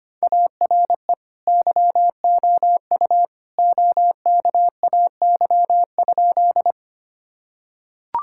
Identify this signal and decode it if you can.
Morse code, heard as a keyed tone, about 25 wpm, 700 Hz. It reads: ARE YOU OKAY?
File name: practice / 25wpm / Sentences with 2-Word Phrases - 25wpm